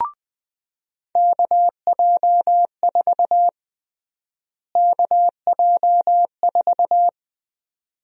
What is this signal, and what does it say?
Morse code, heard as a keyed tone, about 20 wpm, 700 Hz. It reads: KJ4 KJ4
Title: USA Call Sign Prefixes - Repeated 2x - 20wpm